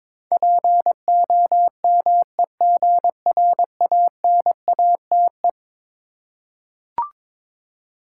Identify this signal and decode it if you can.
Morse code, heard as a keyed tone, about 22 wpm, 690 Hz. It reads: POMEGRANATE
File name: Difficult Words with Hints - 22wpm